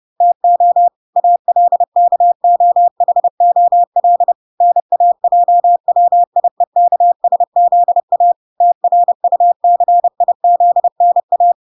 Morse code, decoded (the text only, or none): TO ALKOHOL NAJWIEKSZA TRUCIZNA